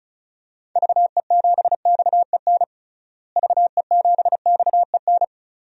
Morse code, heard as a keyed tone, about 36 wpm, 700 Hz. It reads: VE7XEN VE7XEN
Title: DXCC-100 Call Signs - 6 Characters - Repeated 2x - 35wpm